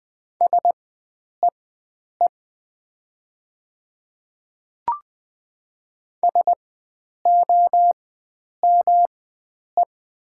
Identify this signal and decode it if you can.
Morse code, heard as a keyed tone, about 20 wpm, spaced as for 9 wpm, 700 Hz. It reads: SEE SOME
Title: 4x Character Spacing - Top 100 Words - 20wpm